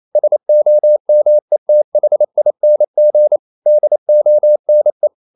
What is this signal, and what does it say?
Morse code, heard as a keyed tone, about 28 wpm, 600 Hz. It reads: SOMETHING DONE
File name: CW / Practice / EnglishSayings28